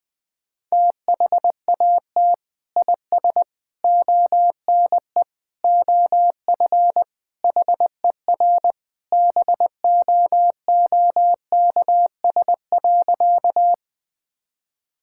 Morse code, heard as a keyed tone, about 20 wpm, 700 Hz. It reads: THAT IS ONE OF HER BOOKS.
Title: Sentences with 2-Word Phrases - Spoken Beginning - 20wpm